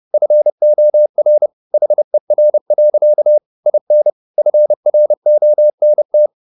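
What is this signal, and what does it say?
Morse code, heard as a keyed tone, about 30 wpm, 600 Hz. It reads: FOR HER. IN FRONT
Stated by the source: Morse code at 30 wpm, the novel Never by Ken Follett in English, from CW med Gnister.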